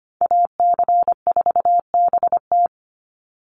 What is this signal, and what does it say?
Morse code, heard as a keyed tone, about 25 wpm, 700 Hz. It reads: AC4BT
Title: Call Signs Repeated 3x - 25wpm